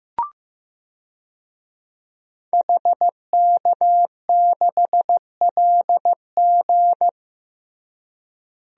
Morse code, heard as a keyed tone, about 15 wpm, 700 Hz. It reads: HK6LG
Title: DXCC-100 Call Signs - 5 Characters - Repeated 3x - 15wpm